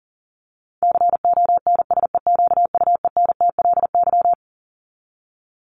Morse code, heard as a keyed tone, about 40 wpm, 700 Hz. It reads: CONSEQUENTLY
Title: LY Suffix - Top 100 Words with Base Words 40wpm